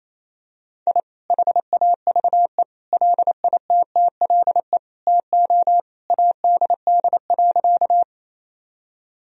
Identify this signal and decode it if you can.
Morse code, heard as a keyed tone, about 28 wpm, 700 Hz. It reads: I HAVE LITTLE TO ADD.
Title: Sentences from Top 1000 Words - Review Part 1 - 28wpm